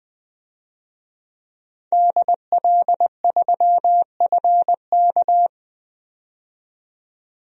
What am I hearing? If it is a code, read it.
Morse code, heard as a keyed tone, about 20 wpm, 700 Hz. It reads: DL3FK
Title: DXCC-100 Call Signs - 5 Characters - 20wpm